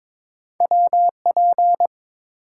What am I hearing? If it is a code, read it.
Morse code, heard as a keyed tone, about 22 wpm, 700 Hz. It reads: WP
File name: practice / 22wpm / Two Letter-Number Combinations - 22wpm